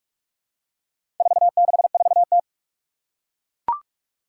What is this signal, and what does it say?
Morse code, heard as a keyed tone, about 41 wpm, 700 Hz. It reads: 464T
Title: Four Challenging Letter-Number Combinations 45wpm